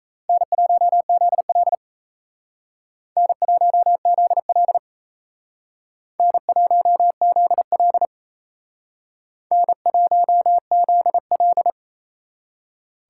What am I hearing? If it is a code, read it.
Morse code, heard as a keyed tone, about 38 wpm, 700 Hz. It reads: N1ZL N1ZL N1ZL N1ZL